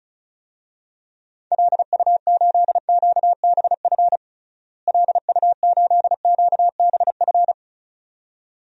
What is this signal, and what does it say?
Morse code, heard as a keyed tone, about 34 wpm, 700 Hz. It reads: LU8QBF LU8QBF
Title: DXCC-100 Call Signs - 6 Characters - Repeated 2x - 35wpm